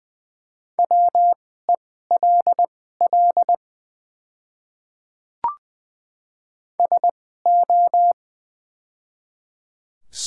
Morse code, heard as a keyed tone, about 20 wpm, spaced as for 14 wpm, 700 Hz. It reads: WELL SO
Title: 2x Character Spacing - Top 200 Words - 20wpm